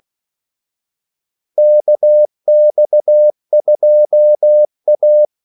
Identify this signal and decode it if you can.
Morse code, heard as a keyed tone, about 16 wpm, 600 Hz. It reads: KX2A
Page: Kaldesignaler 16 wpm